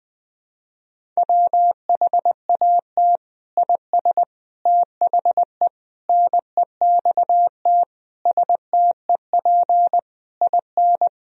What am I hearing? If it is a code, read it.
Morse code, heard as a keyed tone, about 20 wpm, 700 Hz. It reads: WHAT IS THE NEXT STEP IN
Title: Sentences from Top 1000 Words - Review Part 4 - 20wpm